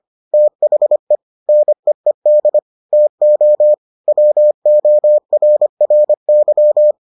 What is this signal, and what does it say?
Morse code, heard as a keyed tone, about 25 wpm, 600 Hz. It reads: THE NEED TO WORRY